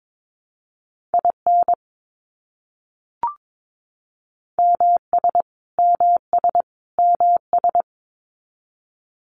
Morse code, heard as a keyed tone, about 22 wpm, 700 Hz. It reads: IN MS MS MS
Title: US States Repeated 3x 22wpm